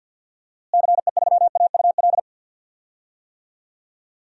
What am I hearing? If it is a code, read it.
Morse code, heard as a keyed tone, about 45 wpm, 700 Hz. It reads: CE3AUL
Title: DXCC-100 Call Signs - 6 Characters - Repeated 2x - 50wpm